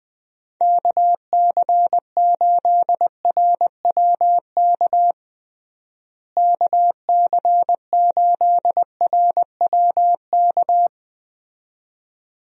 Morse code, heard as a keyed tone, about 20 wpm, 700 Hz. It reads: KC8RWK KC8RWK
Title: USA Call Signs - 6 Characters - Repeated 2x - 20wpm